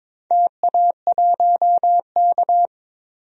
Morse code, heard as a keyed tone, about 22 wpm, 700 Hz. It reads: TA1K
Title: DXCC-100 Call Signs - 4 Characters - 22wpm